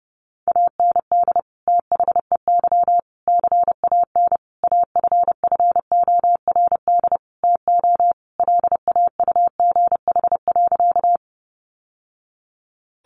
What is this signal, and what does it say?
Morse code, heard as a keyed tone, about 30 wpm, 700 Hz. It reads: AND THEY CAN AFFORD TO LAUGH.
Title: Sentences from Top 1100 Words 30wpm